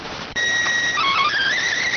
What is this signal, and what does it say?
SSTV (slow-scan television), PD 290.